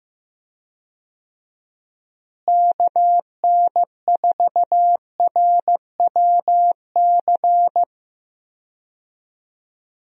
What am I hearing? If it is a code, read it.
Morse code, heard as a keyed tone, about 15 wpm, 700 Hz. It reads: KN4RWC
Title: USA Call Signs - 6 Characters - Repeated 2x - 15wpm